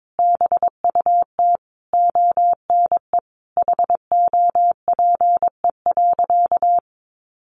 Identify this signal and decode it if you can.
Morse code, heard as a keyed tone, about 22 wpm, 700 Hz. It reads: BUT ONE HOPE.
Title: Sentences from Top 1000 Words 22wpm